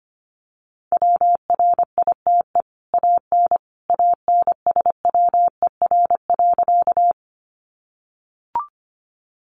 Morse code, heard as a keyed tone, about 25 wpm, 700 Hz. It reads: WRITE AN ANSWER.
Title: Sentences from Top 300 Words 25wpm